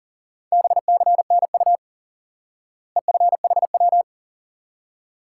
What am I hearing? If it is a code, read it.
Morse code, heard as a keyed tone, about 37 wpm, 700 Hz. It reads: BCNU EFHW
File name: IQE Course 3 - Lesson 89 - YAGI - Two Elements - 40wpm